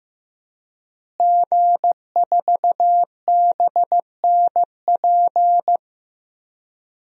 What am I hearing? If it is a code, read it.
Morse code, heard as a keyed tone, about 15 wpm, 700 Hz. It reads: G4BNP
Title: DXCC-100 Call Signs - 5 Characters - Repeated 3x - 15wpm